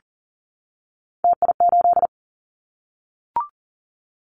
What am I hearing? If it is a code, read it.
Morse code, heard as a keyed tone, about 40 wpm, 700 Hz. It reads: TI8
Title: Callsign Prefixes for Top 100 DXCC - 40wpm